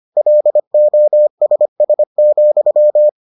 Morse code, heard as a keyed tone, about 25 wpm, 600 Hz. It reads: LOSS,